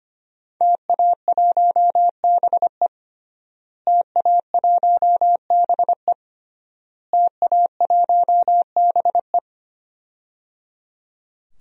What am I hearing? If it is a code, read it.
Morse code, heard as a keyed tone, about 25 wpm, 700 Hz. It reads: TA1BE TA1BE TA1BE
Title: DXCC-100 Call Signs - 5 Characters - Repeated 3x - 25wpm